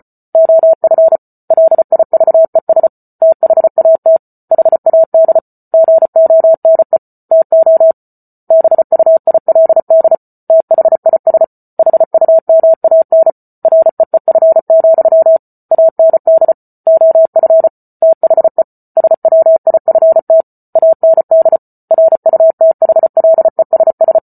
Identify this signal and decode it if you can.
Morse code, heard as a keyed tone, about 34 wpm, 650 Hz. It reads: OF LIVES THAT HAD GONE TO BUILD THIS HUMAN REEF, AND OF THE SWIFT AND RUTHLESS